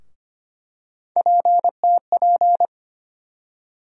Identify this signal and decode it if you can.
Morse code, heard as a keyed tone, about 25 wpm, 700 Hz. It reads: PTP
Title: USA Call Sign Postfixes - 3 Letters 25wpm